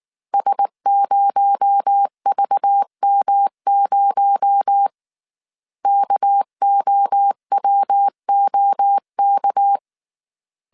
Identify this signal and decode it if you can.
Morse code, heard as a keyed tone, about 19 wpm, 800 Hz. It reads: S0VM0 KOWOK